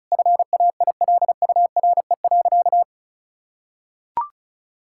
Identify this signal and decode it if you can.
Morse code, heard as a keyed tone, about 36 wpm, 700 Hz. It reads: FAILURE.
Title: Sentences from Top 1000 Words - Review Part 6 - 35wpm